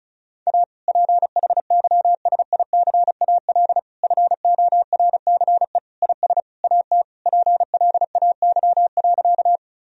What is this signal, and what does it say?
Morse code, heard as a keyed tone, about 36 wpm, 700 Hz. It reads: A PHYSICAL FORCE IS AT PLAY.
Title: Sentences from Top 1000 Words - Review Part 4 - 35wpm